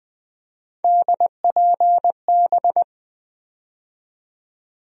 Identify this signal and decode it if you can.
Morse code, heard as a keyed tone, about 20 wpm, 700 Hz. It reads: DPB
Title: USA Call Sign Postfixes - 3 Letters 20wpm